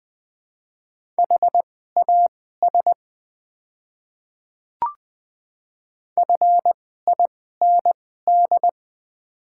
Morse code, heard as a keyed tone, about 20 wpm, spaced as for 14 wpm, 700 Hz. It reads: HAS FIND
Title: 2x Character Spacing - Top 100 Words - 20wpm